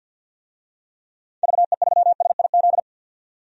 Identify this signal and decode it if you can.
Morse code, heard as a keyed tone, about 49 wpm, 700 Hz. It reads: VE3SIZ
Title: DXCC-100 Call Signs - 6 Characters - 50wpm